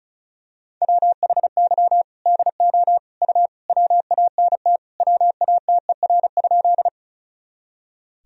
Morse code, heard as a keyed tone, about 34 wpm, 700 Hz. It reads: WHY DO U WANT WATER?
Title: Sentences from Top 200 Words and Abbreviations 35wpm